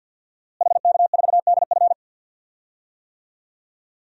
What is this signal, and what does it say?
Morse code, heard as a keyed tone, about 46 wpm, 700 Hz. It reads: HK4DF
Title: DXCC-100 Call Signs - 5 Characters - Repeated 2x - 50wpm